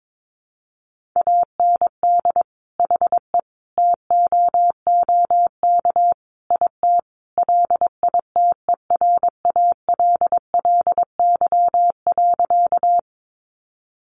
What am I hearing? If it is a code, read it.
Morse code, heard as a keyed tone, about 22 wpm, 700 Hz. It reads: AND HE TOOK IT LITERALLY.
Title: Sentences from Top 1400 Words 22wpm